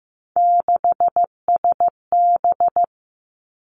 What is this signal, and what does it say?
Morse code, heard as a keyed tone, about 15 wpm, 700 Hz. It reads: 6SB
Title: Three Challenging Letter-Number Combinations 15wpm